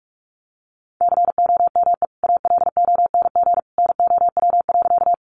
Morse code, heard as a keyed tone, about 44 wpm, 700 Hz. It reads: COME ALONG NOW.
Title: Sentences from Top 700 Words 45wpm